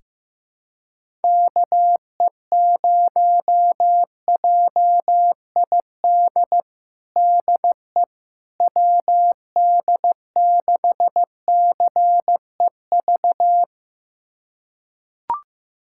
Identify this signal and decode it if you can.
Morse code, heard as a keyed tone, about 16 wpm, 700 Hz. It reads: KE0JID DE WD6CEV
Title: USA Callsign DE Callsign - 6 Characters - 15wpm